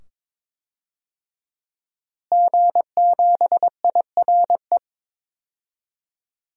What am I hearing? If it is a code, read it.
Morse code, heard as a keyed tone, about 22 wpm, 700 Hz. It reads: G7IRE
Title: DXCC-100 Call Signs - 5 Characters - Repeated 2x - 22wpm